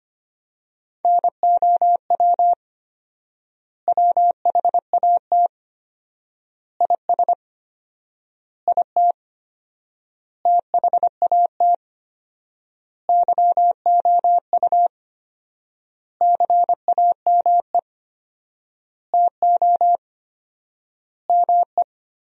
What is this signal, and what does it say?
Morse code, heard as a keyed tone, about 25 wpm, 700 Hz. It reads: NOW WHAT IS IT THAT YOU CAME TO ME
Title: 4x Word-Spacing - Sentences from Top 200 Words - 25wpm